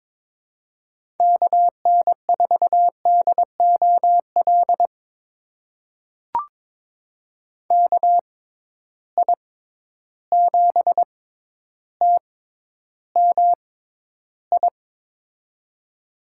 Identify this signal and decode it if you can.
Morse code, heard as a keyed tone, about 22 wpm, spaced as for 11 wpm, 700 Hz. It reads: KN4DOL KI7TMI
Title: USA Call Signs - 6 Characters - 6x Character Spacing - 22wpm